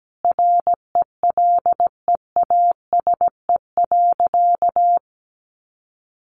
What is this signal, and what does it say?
Morse code, heard as a keyed tone, about 17 wpm, 700 Hz. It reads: RELEASE.
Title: A Sound Of Thunder 17wpm